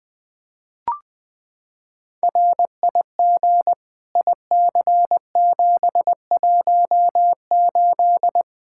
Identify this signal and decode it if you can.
Morse code, heard as a keyed tone, about 20 wpm, 700 Hz. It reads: RIG IC718
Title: Practice QSO - Exchange 2 - Info - 20wpm